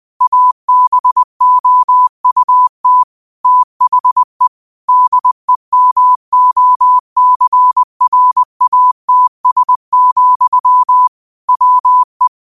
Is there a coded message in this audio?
Yes: Morse code from 0.2 to 12.3 s, heard as a keyed tone, about 20 wpm, 1000 Hz. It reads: ABOUT THE DEMOCRATS, WE